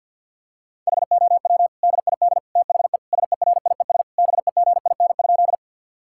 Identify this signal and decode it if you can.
Morse code, heard as a keyed tone, about 50 wpm, 700 Hz. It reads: HOW DID THE SERIES BEGIN?
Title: Sentences from Top 1000 Words - Review Part 5 - 50wpm